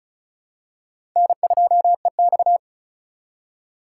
Morse code, heard as a keyed tone, about 34 wpm, 700 Hz. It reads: N2EX
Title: USA Call Signs - 4 Characters - 2x Character Spacing - 35wpm